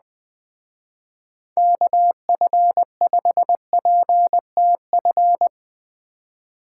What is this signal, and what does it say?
Morse code, heard as a keyed tone, about 20 wpm, 700 Hz. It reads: KF5PTF
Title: Speed Racing - USA 2x3 Call Signs 17wpm